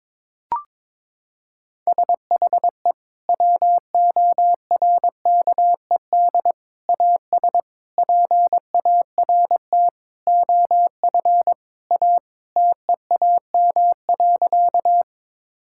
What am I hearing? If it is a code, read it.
Morse code, heard as a keyed tone, about 22 wpm, 700 Hz. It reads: SHE WORKED AS PART OF A TEAM.
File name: practice / 22wpm / Sentences with 4-Word Phrases - 22wpm